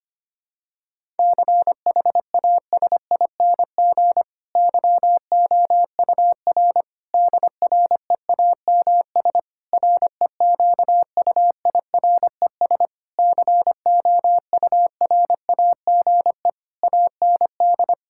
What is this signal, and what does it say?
Morse code, heard as a keyed tone, about 25 wpm, 700 Hz. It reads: CHASING YOUR DREAMS REQUIRES COURAGE AND
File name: practice / 25wpm / Difficult Sentences with Hints - 25wpm